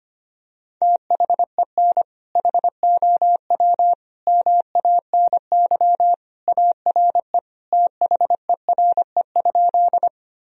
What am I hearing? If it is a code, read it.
Morse code, heard as a keyed tone, about 25 wpm, 700 Hz. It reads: THEN HOW MANY ARE THERE?
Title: Sentences from Top 1000 Words - Review Part 1 - 25wpm